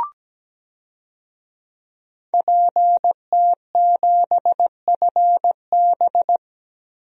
Morse code, heard as a keyed tone, about 17 wpm, 700 Hz. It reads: PT7FB
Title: DXCC-100 Call Signs - 5 Characters - Repeated 2x - 17wpm